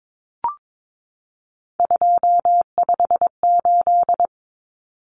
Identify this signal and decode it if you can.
Morse code, heard as a keyed tone, about 22 wpm, 700 Hz. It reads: 258
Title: Numbers - 3-Digit 22wpm